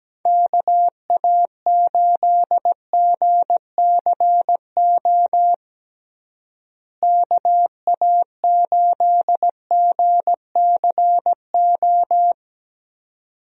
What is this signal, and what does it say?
Morse code, heard as a keyed tone, about 17 wpm, 700 Hz. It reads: KA8GCO KA8GCO
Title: USA Call Signs - 6 Characters - Repeated 2x - 17wpm